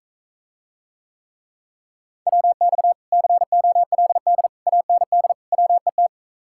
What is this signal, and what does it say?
Morse code, heard as a keyed tone, about 45 wpm, 700 Hz. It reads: WX COLD AND WET